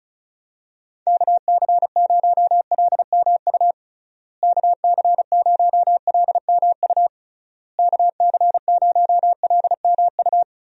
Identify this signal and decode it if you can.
Morse code, heard as a keyed tone, about 34 wpm, 700 Hz. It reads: KC0LMU KC0LMU KC0LMU
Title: USA Call Signs - 6 Characters - Repeated 3x - 35wpm